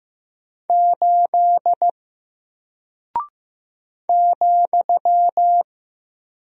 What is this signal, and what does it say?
Morse code, heard as a keyed tone, about 15 wpm, 700 Hz. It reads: 8 ,
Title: Single Character (Letter:Number:Punctuation:Prosign) Alt 15wpm